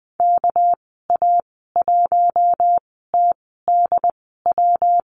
Morse code, heard as a keyed tone, about 20 wpm, spaced as for 14 wpm, 700 Hz. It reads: KA1TDW
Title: 245 Lesson 39g - Letter z - Call Signs 20wpm